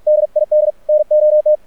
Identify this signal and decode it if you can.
Morse code, heard as a keyed tone, about 46 wpm, 610 Hz. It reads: DID NOT
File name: archived / CRSnet